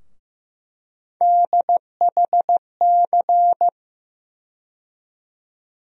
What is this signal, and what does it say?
Morse code, heard as a keyed tone, about 15 wpm, 700 Hz. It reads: DHC
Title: USA Call Sign Postfixes - 3 Letters 15wpm